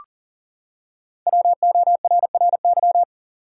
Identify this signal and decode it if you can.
Morse code, heard as a keyed tone, about 40 wpm, 700 Hz. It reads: WORRY
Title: Five Letter Words - 40wpm